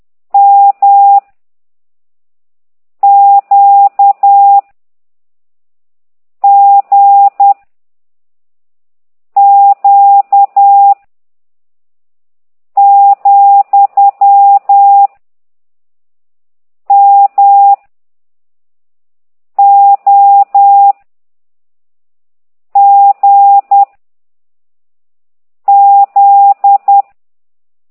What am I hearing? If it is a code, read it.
Morse code, heard as a keyed tone, about 10 wpm, 800 Hz. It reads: M Q G Q , M O G Z